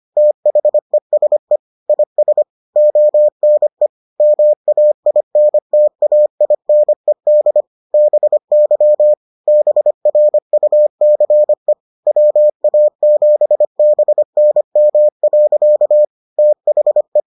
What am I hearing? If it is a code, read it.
Morse code, heard as a keyed tone, about 25 wpm, 600 Hz. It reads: THESE IS ONE MAINTAINED BY BRUCE WA7BNM. THE